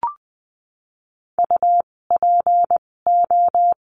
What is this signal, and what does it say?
Morse code, heard as a keyed tone, about 20 wpm, spaced as for 16 wpm, 700 Hz. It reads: UPO
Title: 256 Lesson 42a - Words 12-20wpm